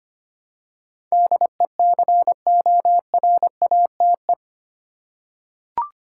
Morse code, heard as a keyed tone, about 25 wpm, 700 Hz. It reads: DECORATE
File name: 8 Letter Words - First 3 Letters Given - 25wpm